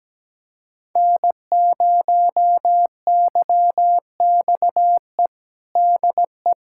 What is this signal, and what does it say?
Morse code, heard as a keyed tone, about 17 wpm, 700 Hz. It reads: N0YXE DE
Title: USA Callsign DE Callsign 17wpm